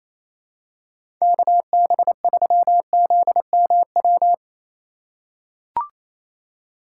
Morse code, heard as a keyed tone, about 27 wpm, 700 Hz. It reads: KB3ZMW
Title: USA Call Signs - 6 Characters - 3x Character Spacing - 28wpm